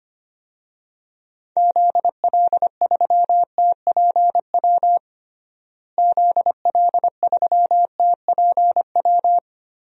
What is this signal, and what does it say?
Morse code, heard as a keyed tone, about 25 wpm, 700 Hz. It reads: ZL3TPW ZL3TPW
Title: DXCC-100 Call Signs - 6 Characters - Repeated 3x - 25wpm